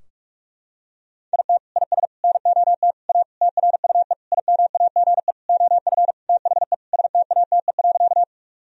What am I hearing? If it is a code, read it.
Morse code, heard as a keyed tone, about 47 wpm, 700 Hz. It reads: IT IS NOT A TRUE IMAGE OF THE STATE.
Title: Sentences from Top 1000 Words - Review Part 4 - 45wpm